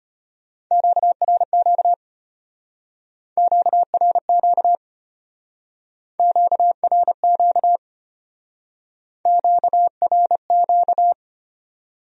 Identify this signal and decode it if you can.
Morse code, heard as a keyed tone, about 32 wpm, 700 Hz. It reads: QRQ QRQ QRQ QRQ